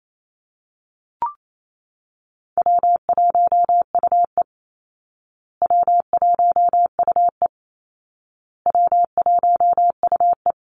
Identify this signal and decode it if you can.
Morse code, heard as a keyed tone, about 28 wpm, 700 Hz. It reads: W1UE W1UE W1UE
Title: Easy Call Signs Repeated 3x - 28wpm